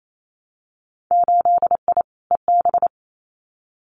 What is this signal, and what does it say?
Morse code, heard as a keyed tone, about 28 wpm, 700 Hz. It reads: 8I EB
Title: Winter Field Day Exchanges Encore 28wpm